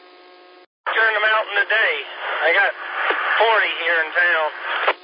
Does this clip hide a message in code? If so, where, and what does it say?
Yes: Emergency Alert System (SAME) tones from 0.9 to 4.3 s, heard as harsh buzzing bursts, with speech.